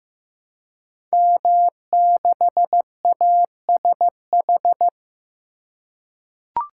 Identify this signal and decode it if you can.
Morse code, heard as a keyed tone, about 15 wpm, 700 Hz. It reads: M6ASH
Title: Code Groups - 5 Letter:Numbers - 15wpm